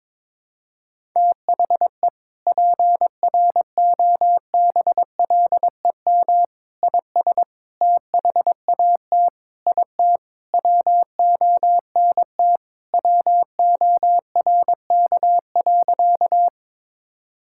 Morse code, heard as a keyed tone, about 22 wpm, 700 Hz. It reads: THE PROBLEM IS THAT IT WONT WORK.
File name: practice / 22wpm / Sentences with 3-Word Phrases - 22wpm